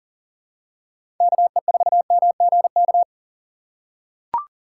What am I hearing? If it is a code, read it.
Morse code, heard as a keyed tone, about 40 wpm, 700 Hz. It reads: KE4MGK
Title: USA Call Signs - 6 Characters - 3x Character Spacing - 40wpm